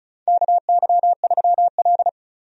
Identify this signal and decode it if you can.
Morse code, heard as a keyed tone, about 35 wpm, 700 Hz. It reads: KY3L
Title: USA Call Signs - 4 Characters - 35wpm